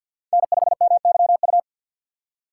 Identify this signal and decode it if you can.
Morse code, heard as a keyed tone, about 46 wpm, 690 Hz. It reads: N5MYU
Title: USA Call Signs - 5 Characters - 6x Character Spacing - 50wpm